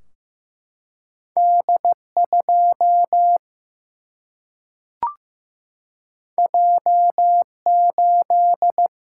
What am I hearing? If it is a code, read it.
Morse code, heard as a keyed tone, about 15 wpm, 700 Hz. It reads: D2 J8
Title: Single-Letter Single-Number 15wpm